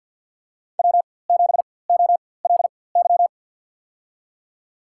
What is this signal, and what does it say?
Morse code, heard as a keyed tone, about 48 wpm, spaced as for 23 wpm, 700 Hz. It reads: W7OLY
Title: USA Call Signs - 4x Character Spacing - 50wpm